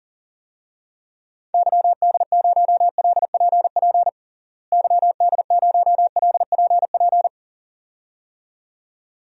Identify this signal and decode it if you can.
Morse code, heard as a keyed tone, about 40 wpm, 700 Hz. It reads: YD0LPP YD0LPP
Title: DXCC-100 Call Signs - 6 Characters - Repeated 2x - 40wpm